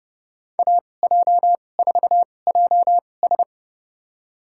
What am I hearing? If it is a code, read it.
Morse code, heard as a keyed tone, about 30 wpm, spaced as for 22 wpm, 700 Hz. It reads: AJ4JS
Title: USA Call Signs - 5 Characters - 2x Character Spacing - 30wpm